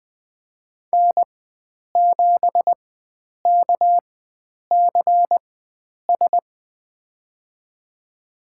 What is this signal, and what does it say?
Morse code, heard as a keyed tone, about 20 wpm, spaced as for 9 wpm, 700 Hz. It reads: N7KCS
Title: USA Call Signs - 4x Character Spacing - 20wpm